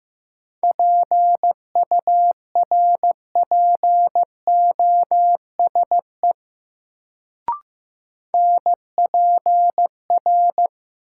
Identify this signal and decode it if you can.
Morse code, heard as a keyed tone, about 15 wpm, 700 Hz. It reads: PURPOSE NPR
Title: Sets of 5 Words - New Word Drill - 15wpm